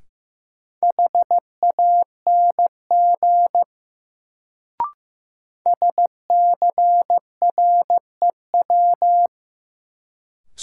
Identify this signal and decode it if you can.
Morse code, heard as a keyed tone, about 15 wpm, 700 Hz. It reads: HANG SCREW
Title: Sentences from Top 1600 Words - New Word Drill Alt - 15wpm